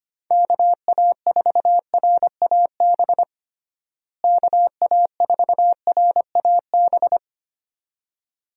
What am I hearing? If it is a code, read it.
Morse code, heard as a keyed tone, about 25 wpm, 700 Hz. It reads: KA4RAB KA4RAB
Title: USA Call Signs - 6 Characters - Repeated 2x - 25wpm